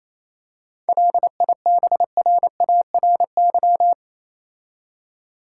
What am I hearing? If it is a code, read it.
Morse code, heard as a keyed tone, about 27 wpm, 700 Hz. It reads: LIBRARY
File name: Easy Words - Speed-Racing with Hints - 22wpm